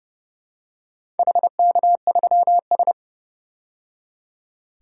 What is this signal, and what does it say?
Morse code, heard as a keyed tone, about 29 wpm, 700 Hz. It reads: HK3S
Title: DXCC-100 Call Signs - 4 Characters - 30wpm